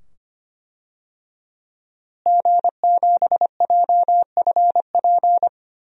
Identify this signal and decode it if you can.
Morse code, heard as a keyed tone, about 25 wpm, 700 Hz. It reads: G7JFP
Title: DXCC-100 Call Signs - 5 Characters - Repeated 2x - 25wpm